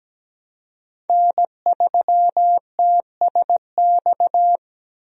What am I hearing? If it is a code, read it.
Morse code, heard as a keyed tone, about 17 wpm, 700 Hz. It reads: N3TSX
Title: USA Call Signs - 5 Characters - Repeated 2x - 17wpm